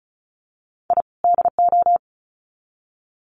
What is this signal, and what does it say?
Morse code, heard as a keyed tone, about 34 wpm, 700 Hz. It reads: I DO
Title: Sets of 2 Words 35wpm